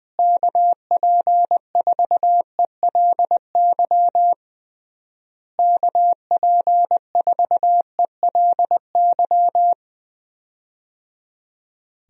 Morse code, heard as a keyed tone, about 20 wpm, 700 Hz. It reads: KP4ELY KP4ELY
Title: USA Call Signs - 6 Characters - Repeated 3x - 20wpm